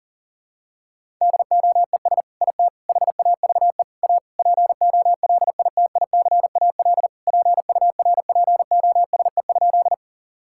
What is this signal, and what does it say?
Morse code, heard as a keyed tone, about 37 wpm, 700 Hz. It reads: DOES IT HAVE A POLITICAL PURPOSE?
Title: Sentences from Top 1000 Words - Review Part 5 - 40wpm